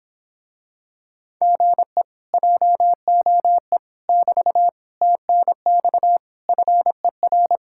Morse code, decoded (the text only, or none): GE JOE <BT> TNX FER